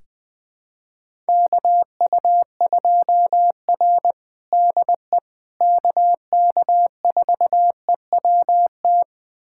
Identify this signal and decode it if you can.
Morse code, heard as a keyed tone, about 20 wpm, 700 Hz. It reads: KU2R DE KK4EWT
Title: USA Callsign DE Callsign 20wpm